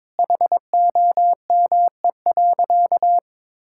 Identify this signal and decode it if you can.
Morse code, heard as a keyed tone, about 22 wpm, 700 Hz. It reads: HOME.
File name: practice / 22wpm / Sentences with 2-Word Phrases - 22wpm